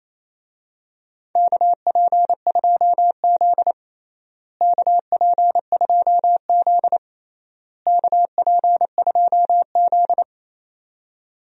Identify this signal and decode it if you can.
Morse code, heard as a keyed tone, about 28 wpm, 700 Hz. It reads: KP2Z KP2Z KP2Z
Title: DXCC-100 Call Signs - 4 Characters - Repeated 3x - 28wpm